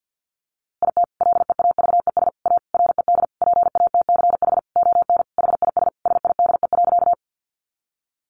Morse code, heard as a keyed tone, about 50 wpm, 700 Hz. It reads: IT LEAVES A RED PATCH ON HIS SIDE.
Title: A Sound Of Thunder 50wpm